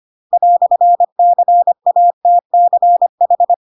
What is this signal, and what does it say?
Morse code, heard as a keyed tone, about 25 wpm, 700 Hz. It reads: "CATCH
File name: Pinocchio / Pinocchio ch3 25 25 ik1hge